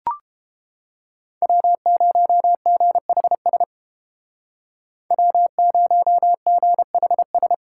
Morse code, heard as a keyed tone, about 30 wpm, 700 Hz. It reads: W0GHS W0GHS